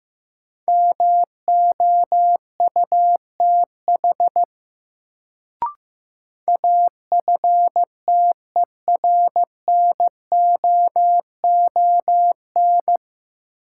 Morse code, heard as a keyed tone, about 15 wpm, 700 Hz. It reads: MOUTH AFTERNOON
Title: Sentences from Top 1500 Words - New Word Drill - 15wpm